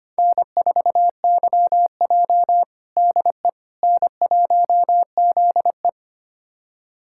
Morse code, heard as a keyed tone, about 25 wpm, 700 Hz. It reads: N4YJ DE N1ZE
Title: USA Callsign DE Callsign - 4 Characters - 25wpm